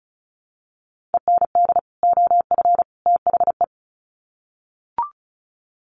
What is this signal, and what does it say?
Morse code, heard as a keyed tone, about 34 wpm, 700 Hz. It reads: END OF THE
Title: Sets of 3 Words 35wpm